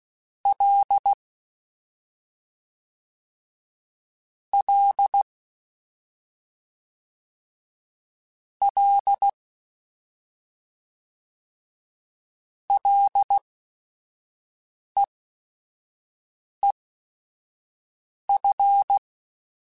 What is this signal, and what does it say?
Morse code, heard as a keyed tone, about 16 wpm, spaced as for 5 wpm, 800 Hz. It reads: L L L LEEF
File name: Short Runs / Short Runs-mp3s / SinglePrefixes @5